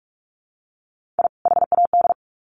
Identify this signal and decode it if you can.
Morse code, heard as a keyed tone, about 44 wpm, 700 Hz. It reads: I HAD
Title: Sets of 2 Words 45wpm